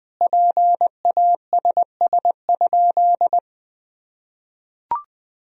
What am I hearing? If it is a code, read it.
Morse code, heard as a keyed tone, about 20 wpm, 700 Hz. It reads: PASS?
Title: Sentences from Top 400 Words and Abbreviations 20wpm